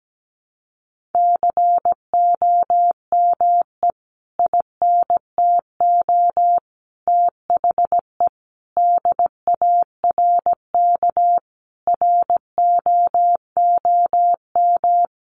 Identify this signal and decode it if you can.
Morse code, heard as a keyed tone, about 17 wpm, 700 Hz. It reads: COME INTO THE DARK ROOM
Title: Sentences from Top 1000 Words 17wpm